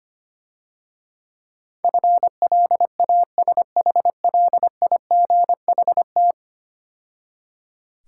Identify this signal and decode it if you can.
Morse code, heard as a keyed tone, about 25 wpm, 700 Hz. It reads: FLASHLIGHT
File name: practice / 25wpm / Difficult Words - Speed-Racing with Hints - 25wpm